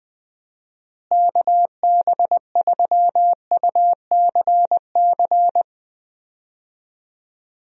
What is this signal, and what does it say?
Morse code, heard as a keyed tone, about 20 wpm, 700 Hz. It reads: KB3UCC
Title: USA Call Signs - 6 Characters - Repeated 3x - 20wpm